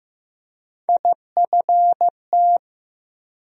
Morse code, heard as a keyed tone, about 15 wpm, 700 Hz. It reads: IFT
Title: USA Call Sign Postfixes - 3 Letters 15wpm